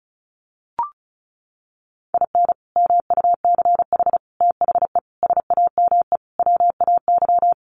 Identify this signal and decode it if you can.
Morse code, heard as a keyed tone, about 34 wpm, 700 Hz. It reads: IN MUCH THE SAME WAY
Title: Sets of 5 Words Encore 35wpm